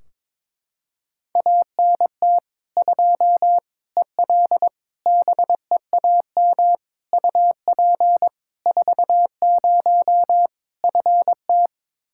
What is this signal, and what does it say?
Morse code, heard as a keyed tone, about 22 wpm, 700 Hz. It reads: ANT 2 EL BEAM UP 40 FT
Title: K1USN SST QSO Practice 22wpm